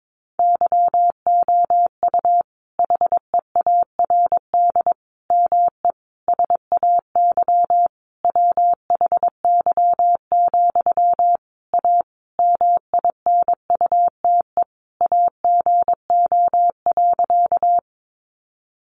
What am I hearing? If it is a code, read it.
Morse code, heard as a keyed tone, about 22 wpm, 700 Hz. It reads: YOU HEARD ME SAY WHY, A MINUTE AGO.
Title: Sentences from Top 500 Words 22wpm